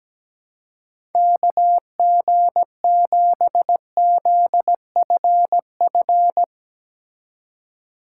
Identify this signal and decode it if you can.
Morse code, heard as a keyed tone, about 18 wpm, 700 Hz. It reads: KG7ZFF
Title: Speed Racing - USA 2x3 Call Signs 17wpm